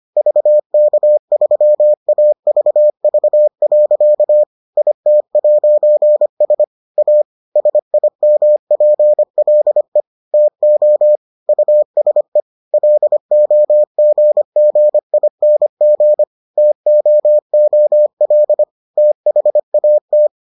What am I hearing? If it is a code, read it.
Morse code, heard as a keyed tone, about 25 wpm, 600 Hz. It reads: VK3AVV. IT'S A SIMPLE TO USE LOGGING TOOL THAT